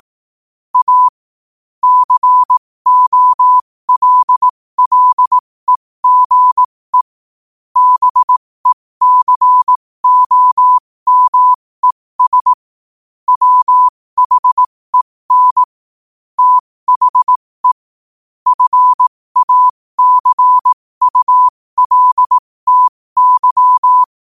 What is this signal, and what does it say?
Morse code, heard as a keyed tone, about 18 wpm, spaced as for 15 wpm, 1000 Hz. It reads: A COLLEGE BECOMES WHEN THE FACULTY
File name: qotdfiles / 15wpm